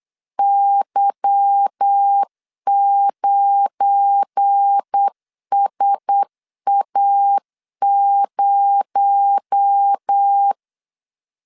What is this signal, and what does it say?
Morse code, heard as a keyed tone, about 8 wpm, 800 Hz. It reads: Y9SA0
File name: LEZIONI / 10 SIGNS / 10C DOT 8wpm 3649